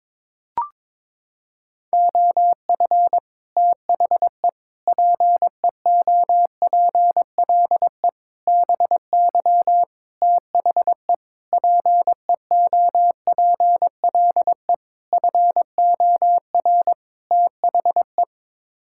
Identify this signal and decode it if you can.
Morse code, heard as a keyed tone, about 21 wpm, 700 Hz. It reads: OF THE PEOPLE BY THE PEOPLE FOR THE
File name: practice / 22wpm / Sets of 8 Words - 22wpm